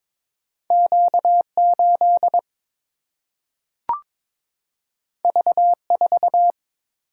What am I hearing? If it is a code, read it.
Morse code, heard as a keyed tone, about 22 wpm, 700 Hz. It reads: Q 8 V 4
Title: Single-Letter Single-Number 22wpm